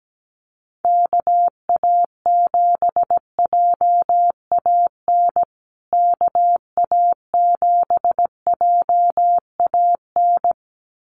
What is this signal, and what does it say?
Morse code, heard as a keyed tone, about 17 wpm, 700 Hz. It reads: KA7JAN KA7JAN
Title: USA Call Signs Repeated 2x 17wpm